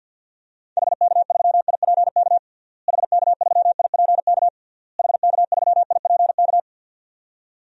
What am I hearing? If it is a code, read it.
Morse code, heard as a keyed tone, about 48 wpm, 700 Hz. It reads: HK3IPK HK3IPK HK3IPK
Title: DXCC-100 Call Signs - 6 Characters - Repeated 3x - 50wpm